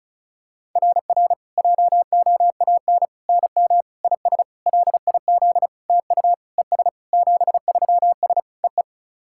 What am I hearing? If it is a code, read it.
Morse code, heard as a keyed tone, about 34 wpm, 690 Hz. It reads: RR JOAN NM IS LIZ TU ES 73S EE
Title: SOTA QSO Practice 35wpm